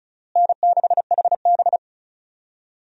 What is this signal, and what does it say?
Morse code, heard as a keyed tone, about 34 wpm, 700 Hz. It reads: N6HB
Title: USA Call Signs - 4 Characters - Repeated 2x - 35wpm